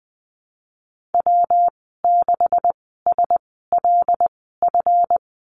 Morse code, heard as a keyed tone, about 20 wpm, spaced as for 15 wpm, 700 Hz. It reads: W6SLF
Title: 154 Lesson 25g - Letter p - Call Signs 20wpm